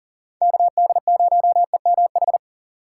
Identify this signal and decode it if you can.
Morse code, heard as a keyed tone, about 40 wpm, 700 Hz. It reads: KD0EMH
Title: USA Call Signs - 4x Character Spacing - 40wpm